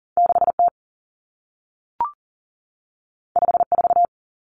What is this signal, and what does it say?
Morse code, heard as a keyed tone, about 40 wpm, 700 Hz. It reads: 6T 54
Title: Common Cut Numbers - 2-Digit 40wpm